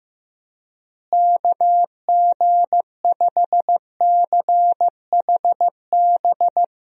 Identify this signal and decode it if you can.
Morse code, heard as a keyed tone, about 15 wpm, 700 Hz. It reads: KG5CHB
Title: USA Call Signs - 6 Characters - Repeated 2x - 15wpm